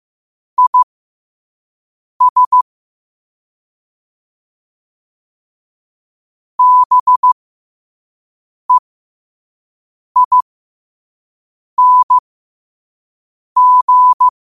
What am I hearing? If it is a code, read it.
Morse code, heard as a keyed tone, about 15 wpm, spaced as for 5 wpm, 1000 Hz. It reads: IS BEING
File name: qotdfiles / 5wpm